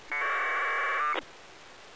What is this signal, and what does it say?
Packet radio (AX.25, as in APRS).